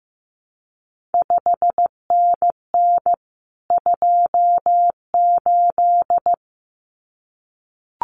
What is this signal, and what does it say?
Morse code, heard as a keyed tone, about 15 wpm, 700 Hz. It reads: 5NN 28
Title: CQ WW DX Contest Exchange Encore - 15wpm